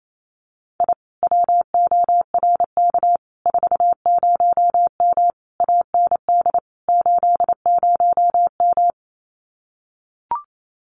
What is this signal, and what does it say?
Morse code, heard as a keyed tone, about 28 wpm, 700 Hz. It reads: I WORK 40M AND 80M
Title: Common QSO Phrases Encore 28wpm